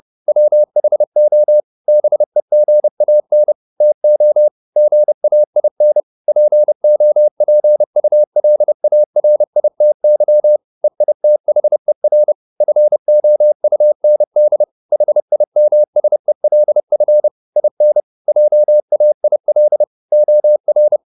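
Morse code, heard as a keyed tone, about 30 wpm, 600 Hz. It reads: WHO BEGAN TO GAIN POPULARITY EITHER FOUND HIMSELF IN JAIL OR
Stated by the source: Morse code at 30 wpm, the novel Never by Ken Follett in English, from CW med Gnister.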